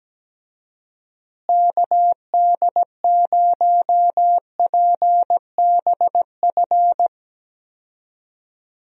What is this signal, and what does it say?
Morse code, heard as a keyed tone, about 17 wpm, 700 Hz. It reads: KD0PBF
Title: USA Call Signs - 6 Characters - Repeated 3x - 17wpm